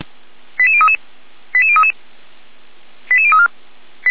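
Five-tone selective call (ZVEI), heard as quick runs of five tones, tones of about 70 ms. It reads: 88033 88033 88034
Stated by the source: ZVEI five-tone call 88190 received by SDR, from the feuerwehr-orlamuende.org (fire brigade Orlamünde) audio archive.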